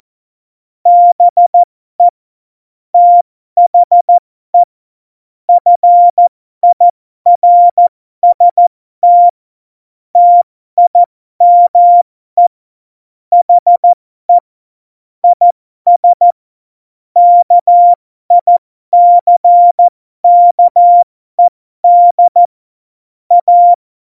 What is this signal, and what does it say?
Morse code, heard as a keyed tone, about 14 wpm, spaced as for 12 wpm, 700 Hz. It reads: BE THE FIRST TIME HE IS KICKED A
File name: Morsekurs / Witze